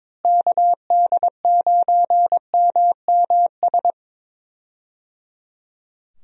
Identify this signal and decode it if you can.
Morse code, heard as a keyed tone, about 22 wpm, 700 Hz. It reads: KD9MMS
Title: USA Call Signs - 6 Characters - Repeated 3x - 22wpm